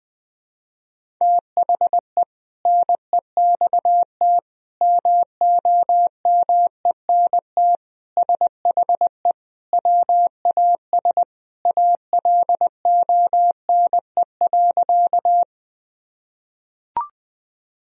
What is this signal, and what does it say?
Morse code, heard as a keyed tone, about 20 wpm, 700 Hz. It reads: THE NEXT MOMENT SHE WAS ALONE.
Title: Sentences from Top 1000 Words - Review Part 1 - 20wpm